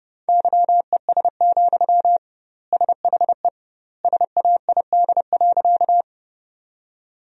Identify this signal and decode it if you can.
Morse code, heard as a keyed tone, about 32 wpm, 700 Hz. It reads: YES, SHE SAID.
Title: 2x Word-Spacing - Sentences from Top 500 Words - 30wpm